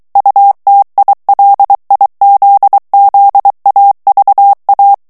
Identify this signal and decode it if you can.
Morse code, heard as a keyed tone, about 23 wpm, 800 Hz. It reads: UTILIZZAVA
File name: morse mp3 / ITALIEN texte clair 1 vitesse 22 mots minute